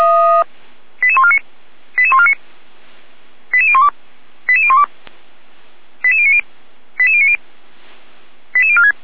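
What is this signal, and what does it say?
Five-tone selective call (ZVEI), heard as quick runs of five tones, tones of about 70 ms. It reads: 88149 88149 88012 88012 88090 88090 88046
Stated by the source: ZVEI five-tone selective call received on an SDR by feuerwehr-orlamuende fire brigade.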